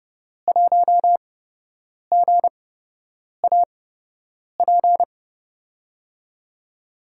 Morse code, heard as a keyed tone, about 32 wpm, spaced as for 8 wpm, 700 Hz. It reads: 1GAP
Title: USA Call Signs - 5 Characters - 8x Character Spacing - 30wpm